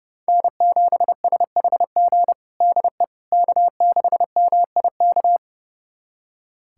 Morse code, heard as a keyed tone, about 29 wpm, 700 Hz. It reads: N7SHG DE K6MIK
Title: USA Callsign DE Callsign - 5 Characters - 30wpm